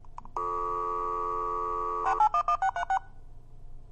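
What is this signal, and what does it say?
DTMF, heard as telephone touch tones, tones of about 80 ms. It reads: 1636
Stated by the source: Sound effect of a modem dialing, logging on and closing in a roomy acoustic.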